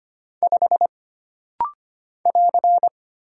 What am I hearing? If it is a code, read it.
Morse code, heard as a keyed tone, about 25 wpm, 700 Hz. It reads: 5 <AR>
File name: Single Character (Letter:Number:Punctuation:Prosign) - Rapid-Fire - 25wpm